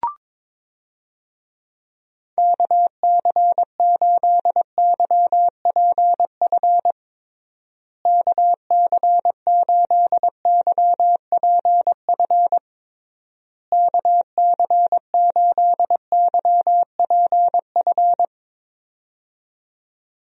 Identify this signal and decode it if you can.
Morse code, heard as a keyed tone, about 22 wpm, 700 Hz. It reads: KC8YPF KC8YPF KC8YPF
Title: USA Call Signs - 6 Characters - Repeated 3x - 22wpm